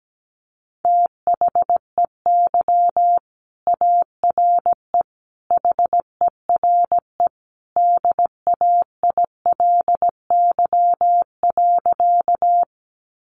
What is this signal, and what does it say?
Morse code, heard as a keyed tone, about 17 wpm, 700 Hz. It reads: THEY ARE HERE DAILY.
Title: Sentences from Top 1100 Words 17wpm